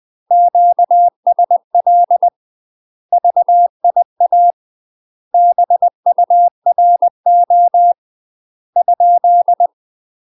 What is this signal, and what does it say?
Morse code, heard as a keyed tone, about 20 wpm, 700 Hz. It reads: QSL VIA BURO ?
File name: UK MASTER QSOs TRAINING / 20WPM / UK MASTER QSOs 0002